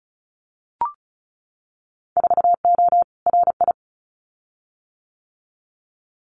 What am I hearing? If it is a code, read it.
Morse code, heard as a keyed tone, about 34 wpm, 700 Hz. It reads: 4O RI
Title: Winter Field Day Exchanges Encore 35wpm